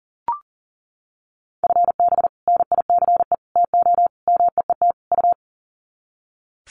Morse code, heard as a keyed tone, about 37 wpm, 700 Hz. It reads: FB NICE TO MEET U
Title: Common QSO Phrases 40wpm